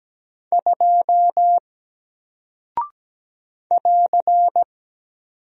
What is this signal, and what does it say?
Morse code, heard as a keyed tone, about 17 wpm, 700 Hz. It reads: 2 <AR>
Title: Single Character (Letter:Number:Punctuation:Prosign) Alt 17wpm